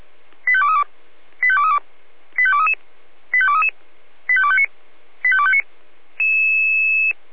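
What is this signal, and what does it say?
Five-tone selective call (ZVEI), heard as quick runs of five tones, tones of about 70 ms. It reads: 85232 85232 85233 85233 85259 85259
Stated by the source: ZVEI five-tone call received by SDR.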